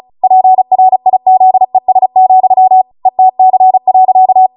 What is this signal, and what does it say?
Morse code, heard as a keyed tone, about 36 wpm, 750 Hz. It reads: PRIZES, ETC.
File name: Morse / 35WPM